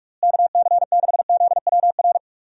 Morse code, heard as a keyed tone, about 46 wpm, 700 Hz. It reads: KC6ZWR
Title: USA Call Signs - 6 Characters - Repeated 3x - 45wpm